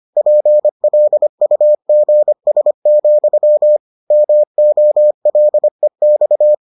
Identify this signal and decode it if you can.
Morse code, heard as a keyed tone, about 25 wpm, 600 Hz. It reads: PLUGS, MOLEX